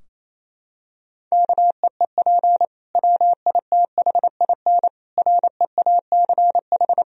Morse code, heard as a keyed tone, about 28 wpm, 700 Hz. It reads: KEEP WITHIN REACH
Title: Sentences from Top 1000 Words - Review Part 4 - 28wpm